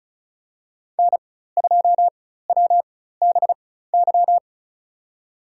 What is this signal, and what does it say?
Morse code, heard as a keyed tone, about 34 wpm, spaced as for 16 wpm, 700 Hz. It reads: N2WBY
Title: USA Call Signs - 4x Character Spacing - 35wpm